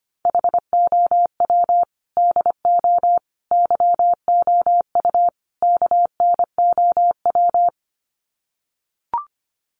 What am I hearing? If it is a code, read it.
Morse code, heard as a keyed tone, about 25 wpm, 700 Hz. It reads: HOW DO YOU KNOW
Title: Sets of 4 Words 25wpm